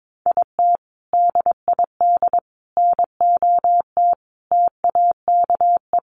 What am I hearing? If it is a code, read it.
Morse code, heard as a keyed tone, about 22 wpm, 700 Hz. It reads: IT DID NOT TAKE
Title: Sentences from Top 600 Words 22wpm